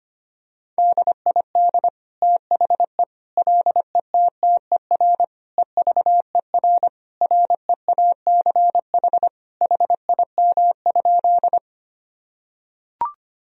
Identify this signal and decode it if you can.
Morse code, heard as a keyed tone, about 25 wpm, 700 Hz. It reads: DID THE LETTER EVER REACH HIM?
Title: Sentences from Top 1000 Words - Review Part 2 - 25wpm